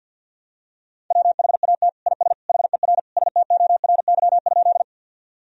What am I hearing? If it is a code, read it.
Morse code, heard as a keyed tone, about 49 wpm, 700 Hz. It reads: WHAT IS HER STORY?
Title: Sentences from Top 1000 Words - Review Part 1 - 50wpm